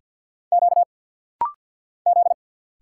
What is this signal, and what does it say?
Morse code, heard as a keyed tone, about 50 wpm, 700 Hz. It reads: Q Z
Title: Single Letter-Number - Rapid-Fire 50wpm